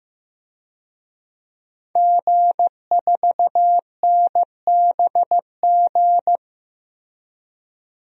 Morse code, heard as a keyed tone, about 15 wpm, 700 Hz. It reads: G4NBG
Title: DXCC-100 Call Signs - 5 Characters - 15wpm